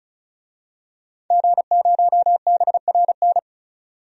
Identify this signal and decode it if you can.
Morse code, heard as a keyed tone, about 36 wpm, 700 Hz. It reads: G0BRN
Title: DXCC-100 Call Signs - 5 Characters - 35wpm